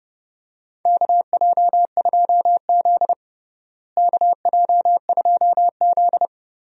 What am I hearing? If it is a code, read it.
Morse code, heard as a keyed tone, about 30 wpm, 700 Hz. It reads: KJ2Z KJ2Z
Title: USA Call Signs - 4 Characters - Repeated 2x - 30wpm